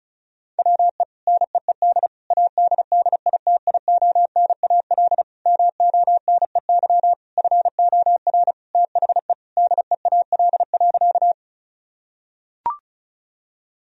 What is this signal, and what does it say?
Morse code, heard as a keyed tone, about 36 wpm, 700 Hz. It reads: WE NEED ADDITIONAL MONEY FOR THE DEAL.
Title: Sentences from Top 1000 Words - Review Part 4 - 35wpm